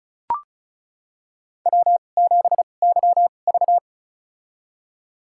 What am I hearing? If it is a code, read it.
Morse code, heard as a keyed tone, about 36 wpm, spaced as for 25 wpm, 700 Hz. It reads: W7YV
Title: USA Call Signs - 4 Characters - 2x Character Spacing - 35wpm